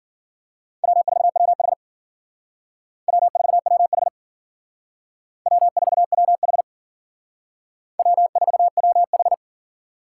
Morse code, heard as a keyed tone, about 51 wpm, 700 Hz. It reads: W4WH W4WH W4WH W4WH